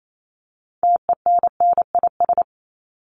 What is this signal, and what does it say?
Morse code, heard as a keyed tone, about 28 wpm, 700 Hz. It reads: TENNIS
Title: Top 1501-1750 Words 28wpm